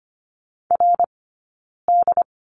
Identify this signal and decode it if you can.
Morse code, heard as a keyed tone, about 25 wpm, 700 Hz. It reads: R D
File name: Lesson 15d - Number 5 - Two Characters 25wpm